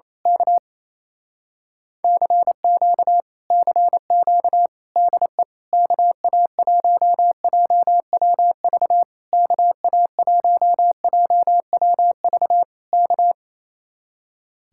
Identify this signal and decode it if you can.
Morse code, heard as a keyed tone, about 26 wpm, 700 Hz. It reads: K CQ CQ DE KA1JWV KA1JWV K
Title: Speed-Racing Common QSO Phrases Encore - 28wpm